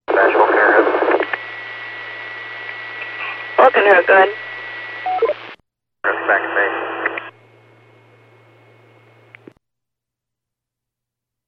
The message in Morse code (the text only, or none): none